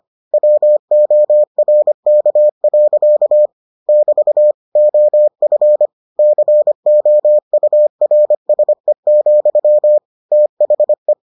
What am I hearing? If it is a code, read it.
Morse code, heard as a keyed tone, about 25 wpm, 600 Hz. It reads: WORK. <BT> OF COURSE, THE